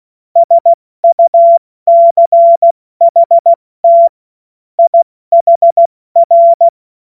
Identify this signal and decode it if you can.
Morse code, heard as a keyed tone, about 16 wpm, 680 Hz. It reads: SUCHT IHR
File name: Morsekurs / Witze / Witz 20. Januar 2026 70 F80